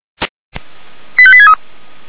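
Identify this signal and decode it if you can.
Five-tone selective call (ZVEI), heard as quick runs of five tones, tones of about 70 ms. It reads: 85863
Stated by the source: Five-tone ZVEI selective call 85864 received on an SDR.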